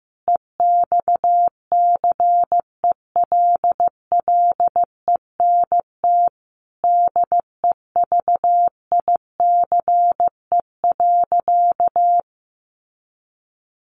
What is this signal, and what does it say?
Morse code, heard as a keyed tone, about 14 wpm, 700 Hz. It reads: EXCELLENT DEVICE.
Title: Sentences from Top 1500 Words 15wpm